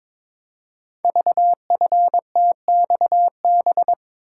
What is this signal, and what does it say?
Morse code, heard as a keyed tone, about 23 wpm, 700 Hz. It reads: VFTXB
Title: Code Groups - 5 Letters - 22wpm